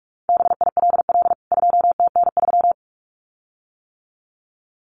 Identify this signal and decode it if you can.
Morse code, heard as a keyed tone, about 44 wpm, 700 Hz. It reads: BILL 2TN3
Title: CWT Exchange 45wpm